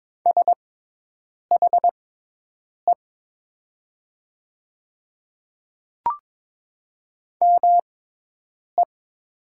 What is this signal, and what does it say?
Morse code, heard as a keyed tone, about 22 wpm, spaced as for 8 wpm, 700 Hz. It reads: SHE ME
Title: 6x Character Spacing - Top 200 Words - 22wpm